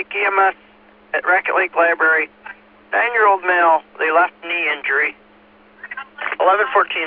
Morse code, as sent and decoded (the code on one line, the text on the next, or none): none